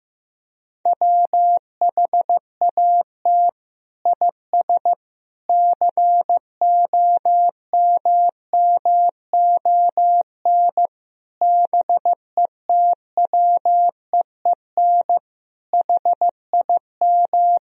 Morse code, heard as a keyed tone, about 15 wpm, 700 Hz. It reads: WHAT IS COMMON BETWEEN HIM
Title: Sentences from Top 1000 Words - Review Part 3 - 15wpm